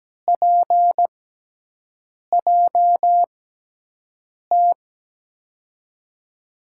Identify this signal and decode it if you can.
Morse code, heard as a keyed tone, about 17 wpm, spaced as for 6 wpm, 700 Hz. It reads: PJT
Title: USA Call Signs - 6x Character Spacing - 17wpm